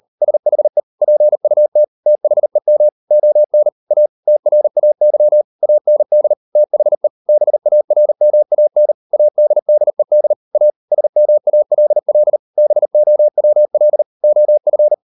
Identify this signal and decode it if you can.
Morse code, heard as a keyed tone, about 38 wpm, 600 Hz. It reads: SHE PUT THEM ON A TRAY AND THE BARMAN ADDED A SMALL BOWL OF